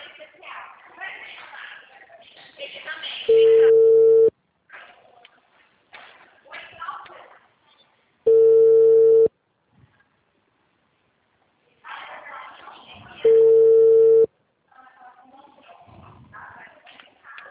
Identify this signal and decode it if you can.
Ringback tone (China, 450 Hz), with speech.
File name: out- -2025- - - .80016